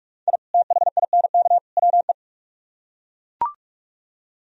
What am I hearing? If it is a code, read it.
Morse code, heard as a keyed tone, about 41 wpm, 700 Hz. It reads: I THINK WE